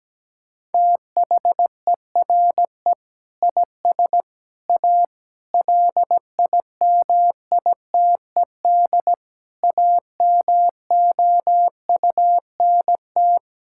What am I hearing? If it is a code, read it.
Morse code, heard as a keyed tone, about 17 wpm, 700 Hz. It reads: THERE IS A LIMITED AMOUNT
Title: Sentences from Top 1000 Words - Review Part 6 - 17wpm